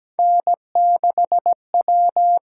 Morse code, heard as a keyed tone, about 17 wpm, 700 Hz. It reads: N6W
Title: USA Call Signs - 4 Characters - 8x Character Spacing - 17wpm